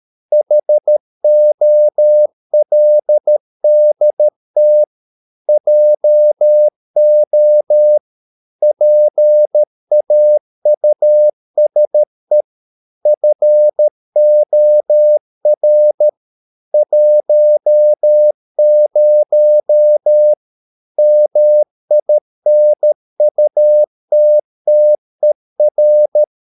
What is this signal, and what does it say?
Morse code, heard as a keyed tone, about 13 wpm, 600 Hz. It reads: HOLDT JO PAUSE FOR 10 MINUTTER
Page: Ferie 13 wpm